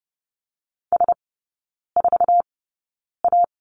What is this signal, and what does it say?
Morse code, heard as a keyed tone, about 30 wpm, spaced as for 9 wpm, 700 Hz. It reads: S4A
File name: Lesson 18e - Letter m - Three Characters 30wpm